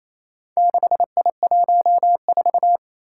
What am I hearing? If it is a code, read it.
Morse code, heard as a keyed tone, about 29 wpm, 700 Hz. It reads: 6I14
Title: Four Challenging Letter-Number Combinations 28wpm